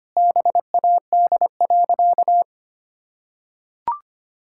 Morse code, heard as a keyed tone, about 25 wpm, 700 Hz. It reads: BAD.
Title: Sentences from Top 1000 Words - Review Part 6 - 25wpm